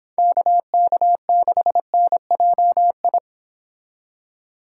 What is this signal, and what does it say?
Morse code, heard as a keyed tone, about 26 wpm, 700 Hz. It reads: KK6NJI